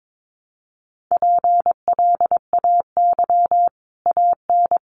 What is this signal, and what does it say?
Morse code, heard as a keyed tone, about 22 wpm, 700 Hz. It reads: PLAY AN
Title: Sets of 5 Words Encore 22wpm